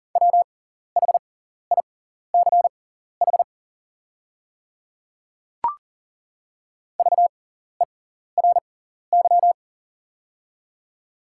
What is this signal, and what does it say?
Morse code, heard as a keyed tone, about 40 wpm, spaced as for 14 wpm, 690 Hz. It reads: WHICH VERY
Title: 6x Character Spacing - Top 200 Words - 40wpm